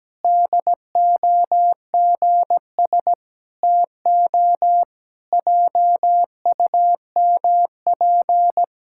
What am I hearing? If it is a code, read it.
Morse code, heard as a keyed tone, about 17 wpm, 700 Hz. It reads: DOGS TO JUMP
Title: Pangram Challenge v2 - 17wpm